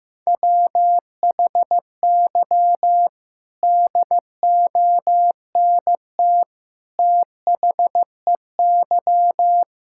Morse code, heard as a keyed tone, about 15 wpm, 700 Hz. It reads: WHY DONT THEY
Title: Sentences from Top 1000 Words - Review Part 5 - 15wpm